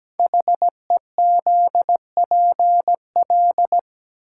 Morse code, heard as a keyed tone, about 17 wpm, 700 Hz. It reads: HEZPL
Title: Code Groups - 5 Letters - 17wpm